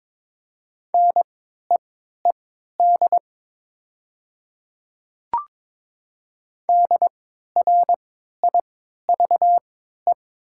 Morse code, heard as a keyed tone, about 22 wpm, spaced as for 13 wpm, 700 Hz. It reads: NEED DRIVE
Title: 3x Character Spacing - Top 500 Words - 22wpm